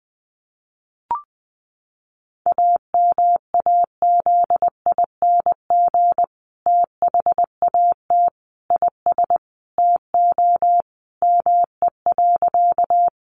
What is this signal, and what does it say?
Morse code, heard as a keyed tone, about 20 wpm, 700 Hz. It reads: AMAZING THAT IS TO ME.
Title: Sentences from Top 1500 Words 20wpm